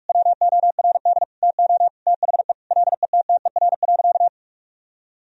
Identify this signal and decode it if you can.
Morse code, heard as a keyed tone, about 47 wpm, 700 Hz. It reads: WORD TO THE LETTER.
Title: Sentences from Top 1000 Words - Review Part 1 - 45wpm